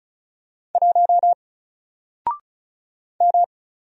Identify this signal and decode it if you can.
Morse code, heard as a keyed tone, about 36 wpm, 700 Hz. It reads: 1 M
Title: Single Letter-Number Alt 35wpm